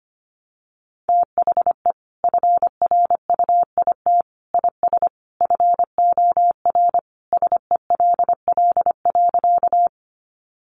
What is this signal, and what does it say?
Morse code, heard as a keyed tone, about 25 wpm, 700 Hz. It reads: THE FRUIT IS FOR SELL.
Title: Sentences from Top 1300 Words 25wpm